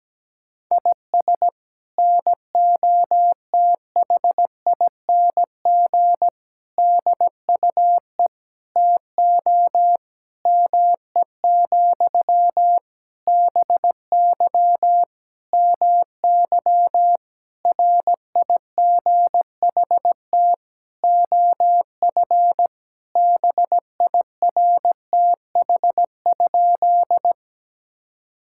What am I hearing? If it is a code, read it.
Morse code, heard as a keyed tone, about 17 wpm, 700 Hz. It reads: IS NOTHING DUE TO ME, BY MY RIGHT OF BIRTH?
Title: Sentences from Top 1000 Words - Review Part 6 - 17wpm